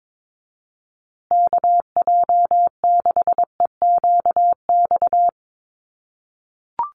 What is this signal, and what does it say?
Morse code, heard as a keyed tone, about 22 wpm, 700 Hz. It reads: KJ6EQX
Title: USA Call Signs Repeated 2x 22wpm